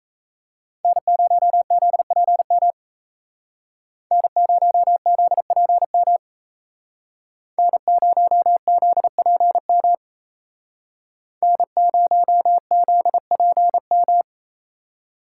Morse code, heard as a keyed tone, about 35 wpm, 700 Hz. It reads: N0ZPM N0ZPM N0ZPM N0ZPM